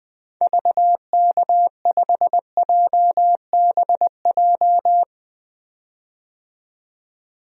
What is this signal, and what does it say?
Morse code, heard as a keyed tone, about 20 wpm, 700 Hz. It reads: VK5JBJ
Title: DXCC-100 Call Signs - 6 Characters - 20wpm